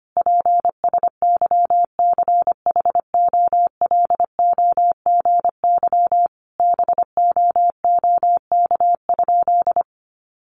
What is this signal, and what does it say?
Morse code, heard as a keyed tone, about 25 wpm, 700 Hz. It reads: PSYCHOLOGY BOOK?
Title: Sentences from Top 1400 Words 25wpm